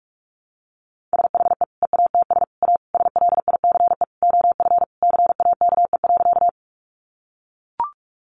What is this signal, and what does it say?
Morse code, heard as a keyed tone, about 44 wpm, 700 Hz. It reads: SHE EATS A SLICE OF CAKE.
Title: Additional Easy Phrases 45wpm